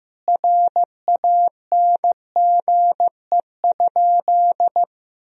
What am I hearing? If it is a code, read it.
Morse code, heard as a keyed tone, about 15 wpm, 700 Hz. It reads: RANGE?
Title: Sentences from Top 1000 Words - Review Part 4 - 15wpm